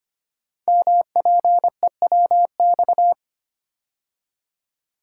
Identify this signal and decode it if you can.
Morse code, heard as a keyed tone, about 25 wpm, 700 Hz. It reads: MPEWX
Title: Code Groups - 5 Letters - 25wpm